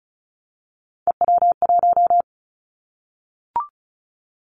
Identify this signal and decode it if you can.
Morse code, heard as a keyed tone, about 34 wpm, 700 Hz. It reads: EW1
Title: Callsign Prefixes for Top 100 DXCC - 35wpm